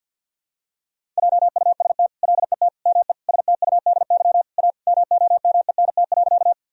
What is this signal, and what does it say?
Morse code, heard as a keyed tone, about 50 wpm, 700 Hz. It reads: JUST LET ME STUDY A MOMENT.
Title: Sentences from Top 1000 Words - Review Part 1 - 50wpm